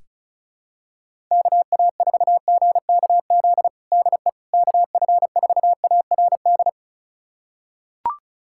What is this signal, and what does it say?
Morse code, heard as a keyed tone, about 36 wpm, 700 Hz. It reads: KA4GKZ DE KF4ARD
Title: USA Callsign DE Callsign - 6 Characters - 35wpm